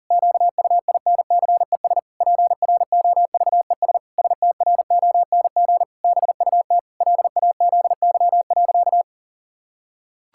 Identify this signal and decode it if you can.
Morse code, heard as a keyed tone, about 40 wpm, 700 Hz. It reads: QUINCES PROVES STRONG BUT LAZY.
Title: Pangram Challenge v2 - 40wpm